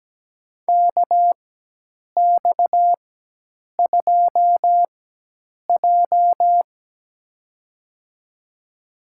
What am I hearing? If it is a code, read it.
Morse code, heard as a keyed tone, about 17 wpm, spaced as for 8 wpm, 700 Hz. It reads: KX2J
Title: USA Call Signs - 4 Characters - 4x Character Spacing - 17wpm